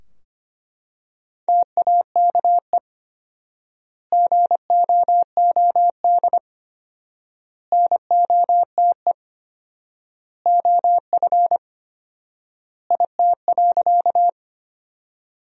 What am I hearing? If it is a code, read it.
Morse code, heard as a keyed tone, about 25 wpm, 700 Hz. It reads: TAKE GOOD NOTE OF IT.
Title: 4x Word-Spacing - Sentences from Top 500 Words - 25wpm